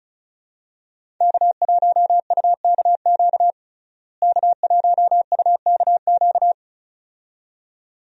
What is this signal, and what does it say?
Morse code, heard as a keyed tone, about 34 wpm, 700 Hz. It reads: K1UKQ K1UKQ
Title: USA Call Signs - 5 Characters - Repeated 2x - 35wpm